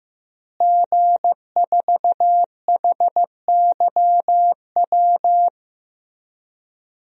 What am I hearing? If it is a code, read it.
Morse code, heard as a keyed tone, about 15 wpm, 700 Hz. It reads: G4HYW
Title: DXCC-100 Call Signs - 5 Characters - Repeated 3x - 15wpm